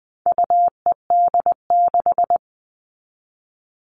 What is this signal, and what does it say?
Morse code, heard as a keyed tone, about 20 wpm, 700 Hz. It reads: UED6
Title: All Cut Numbers - 4-Digit 20wpm